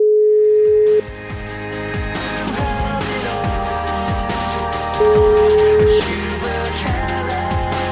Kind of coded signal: telephone call progress tone

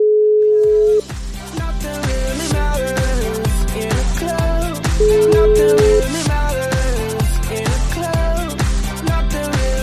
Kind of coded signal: telephone call progress tone

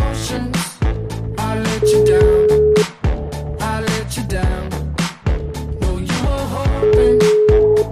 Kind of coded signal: telephone call progress tone

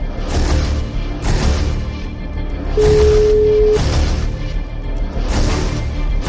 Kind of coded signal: telephone call progress tone

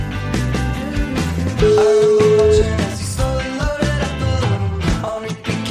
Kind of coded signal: telephone call progress tone

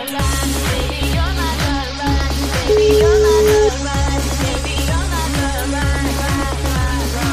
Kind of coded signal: telephone call progress tone

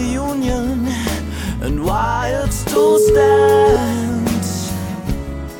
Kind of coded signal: telephone call progress tone

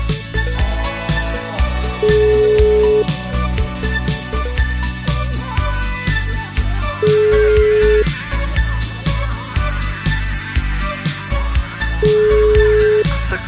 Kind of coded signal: telephone call progress tone

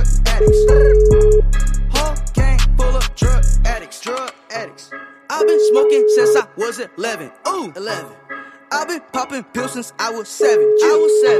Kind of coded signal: telephone call progress tone

